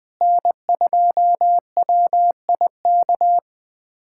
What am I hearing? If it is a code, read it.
Morse code, heard as a keyed tone, about 20 wpm, 700 Hz. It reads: N2WIK
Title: USA Callsign DE Callsign - 5 Characters - 20wpm